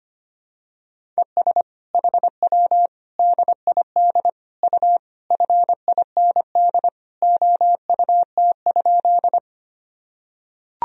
Morse code, heard as a keyed tone, about 25 wpm, 700 Hz. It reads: ES HW DID U FIND OUT?
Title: Sentences from Top 200 Words and Abbreviations 25wpm